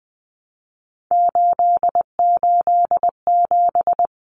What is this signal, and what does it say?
Morse code, heard as a keyed tone, about 20 wpm, 700 Hz. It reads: 887
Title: Common Cut Numbers - 3-Digit 20wpm